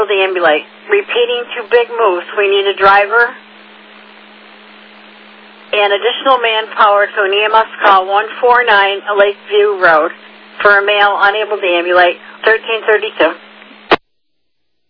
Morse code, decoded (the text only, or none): none